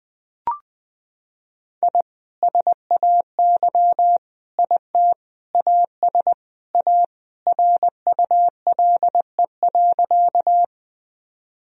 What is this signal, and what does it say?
Morse code, heard as a keyed tone, about 21 wpm, 700 Hz. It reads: I SAY IT AS A RULE.
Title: Sentences from Top 1000 Words - Review Part 2 - 20wpm